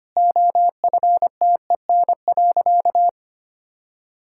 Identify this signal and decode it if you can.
Morse code, heard as a keyed tone, about 25 wpm, 700 Hz. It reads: OFTEN.
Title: Sentences from Top 1000 Words - Review Part 6 - 25wpm